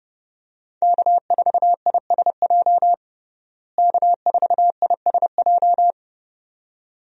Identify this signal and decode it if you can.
Morse code, heard as a keyed tone, about 32 wpm, 700 Hz. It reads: K4ISJ K4ISJ
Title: USA Call Signs - 5 Characters - Repeated 2x - 30wpm